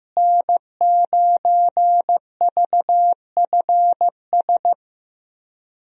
Morse code, heard as a keyed tone, about 15 wpm, 700 Hz. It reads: N9VFS
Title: USA Call Signs - 5 Characters - Repeated 2x - 15wpm